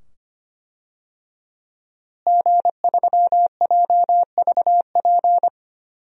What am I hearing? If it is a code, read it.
Morse code, heard as a keyed tone, about 25 wpm, 700 Hz. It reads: G3JVP
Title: DXCC-100 Call Signs - 5 Characters - Repeated 2x - 25wpm